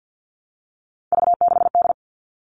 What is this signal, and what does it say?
Morse code, heard as a keyed tone, about 50 wpm, 700 Hz. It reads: V6D
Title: All Cut Numbers - 3-Digit 50wpm